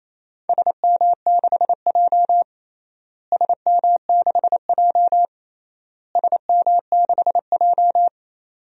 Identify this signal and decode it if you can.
Morse code, heard as a keyed tone, about 28 wpm, 700 Hz. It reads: SM6J SM6J SM6J
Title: DXCC-100 Call Signs - 4 Characters - Repeated 3x - 28wpm